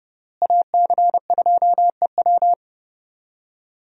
Morse code, heard as a keyed tone, about 30 wpm, 700 Hz. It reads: AC2EW
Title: USA Call Signs - 5 Characters - Repeated 3x - 30wpm